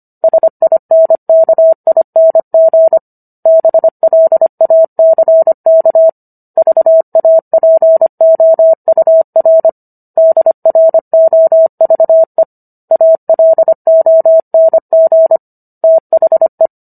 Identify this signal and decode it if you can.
Morse code, heard as a keyed tone, about 25 wpm, 650 Hz. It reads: SINKING BLACK VAPOUR DROVE ALONG THE